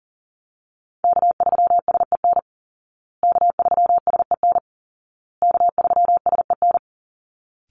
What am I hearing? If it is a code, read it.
Morse code, heard as a keyed tone, about 40 wpm, 700 Hz. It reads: K3SEN K3SEN K3SEN
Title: Call Signs Repeated 2x - 40wpm